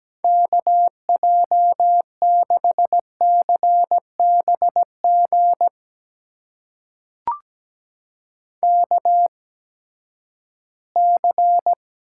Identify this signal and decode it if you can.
Morse code, heard as a keyed tone, about 17 wpm, spaced as for 12 wpm, 700 Hz. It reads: KJ6CBG KC
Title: USA Call Signs - 6 Characters - 8x Character Spacing - 17wpm